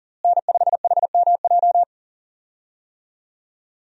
Morse code, heard as a keyed tone, about 40 wpm, 700 Hz. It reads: N5HMJ
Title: USA Call Signs - 5 Characters - Repeated 3x - 40wpm